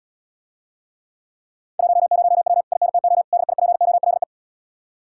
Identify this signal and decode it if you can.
Morse code, heard as a keyed tone, about 80 wpm, 700 Hz. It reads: 10W INTO DIPOLE